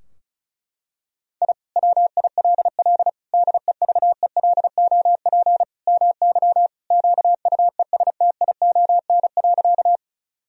Morse code, heard as a keyed tone, about 34 wpm, 700 Hz. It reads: I WILL DEVELOP MY QUESTION.
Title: Sentences from Top 1000 Words - Review Part 2 - 35wpm